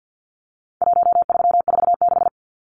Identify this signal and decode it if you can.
Morse code, heard as a keyed tone, about 50 wpm, 700 Hz. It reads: 1346
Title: Numbers - Contest Style - 4-Digit 50wpm